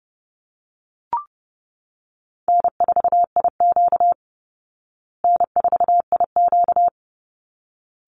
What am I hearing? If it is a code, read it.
Morse code, heard as a keyed tone, about 30 wpm, 700 Hz. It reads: N4IQ N4IQ
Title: Easy Call Signs Repeated 2x - 30wpm